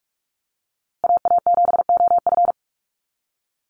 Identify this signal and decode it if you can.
Morse code, heard as a keyed tone, about 44 wpm, 700 Hz. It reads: AA7OF
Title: USA Call Signs Repeated 3x 45wpm